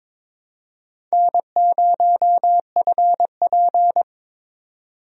Morse code, heard as a keyed tone, about 22 wpm, 700 Hz. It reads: N0FP
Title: USA Call Signs - 4 Characters - 22wpm